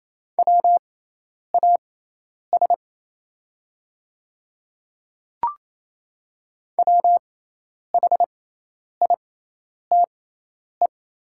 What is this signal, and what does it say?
Morse code, heard as a keyed tone, about 28 wpm, spaced as for 10 wpm, 700 Hz. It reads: WAS WHITE
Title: 6x Character Spacing - Top 300 Words - 28wpm